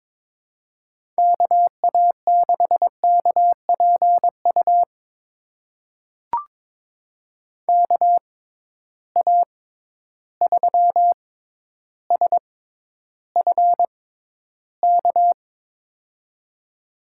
Morse code, heard as a keyed tone, about 22 wpm, spaced as for 11 wpm, 700 Hz. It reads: KA6KPU KA3SFK
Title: USA Call Signs - 6 Characters - 6x Character Spacing - 22wpm